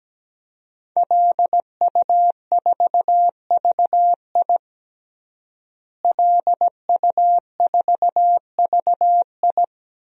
Morse code, heard as a keyed tone, about 17 wpm, 700 Hz. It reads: LU4VI LU4VI
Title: DXCC-100 Call Signs - 5 Characters - Repeated 2x - 17wpm